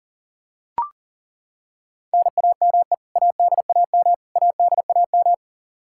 Morse code, heard as a keyed tone, about 40 wpm, 700 Hz. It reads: NAME ADAM ADAM
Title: K1USN SST QSO Practice 40wpm